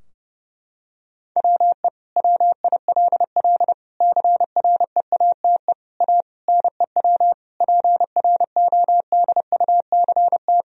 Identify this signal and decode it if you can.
Morse code, heard as a keyed tone, about 30 wpm, 700 Hz. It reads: WE WILL CREATE A NEW PRODUCT
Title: Sentences from Top 1000 Words - Review Part 3 - 30wpm